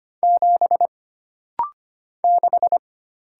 Morse code, heard as a keyed tone, about 25 wpm, 700 Hz. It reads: 7 6
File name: Single Character (Letter:Number:Punctuation:Prosign) - Rapid-Fire - 25wpm